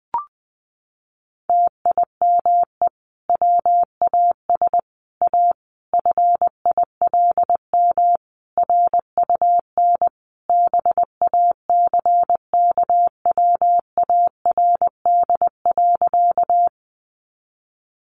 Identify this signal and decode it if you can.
Morse code, heard as a keyed tone, about 19 wpm, 700 Hz. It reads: TIME WAS A FILM RUN BACKWARD.
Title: A Sound Of Thunder 20wpm